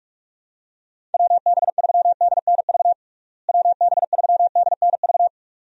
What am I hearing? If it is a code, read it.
Morse code, heard as a keyed tone, about 42 wpm, 700 Hz. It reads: WB3DNV WB3DNV
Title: USA Call Signs - 6 Characters - Repeated 2x - 45wpm